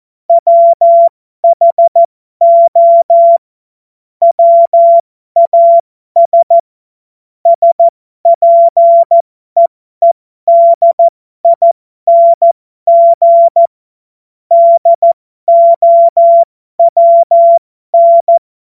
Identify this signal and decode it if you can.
Morse code, heard as a keyed tone, about 14 wpm, spaced as for 12 wpm, 680 Hz. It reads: WHO WAS SPEEDING DOWN
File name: Morsekurs / Witze